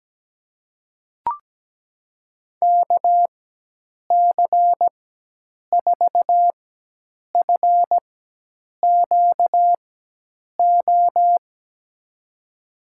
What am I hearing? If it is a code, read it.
Morse code, heard as a keyed tone, about 17 wpm, spaced as for 8 wpm, 700 Hz. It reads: KC4FQO
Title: USA Call Signs - 6 Characters - 4x Character Spacing - 17wpm